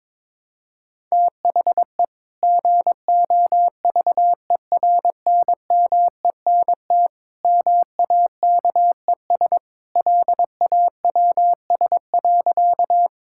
Morse code, heard as a keyed tone, about 22 wpm, 700 Hz. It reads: THE GOVERNMENT MAKES LAWS.
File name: practice / 22wpm / Sentences with 2-Word Phrases - 22wpm